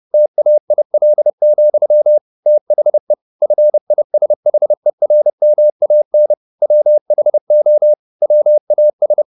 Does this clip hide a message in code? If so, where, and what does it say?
Yes: Morse code from 0.1 to 9.2 s, heard as a keyed tone, about 30 wpm, 600 Hz. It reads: TAIL, THE FISHERMAN WHO WAS